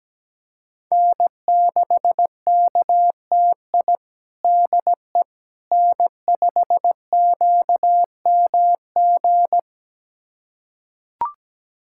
Morse code, heard as a keyed tone, about 17 wpm, 700 Hz. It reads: N6KTI DE N5QMG
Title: USA Callsign DE Callsign - 5 Characters - 17wpm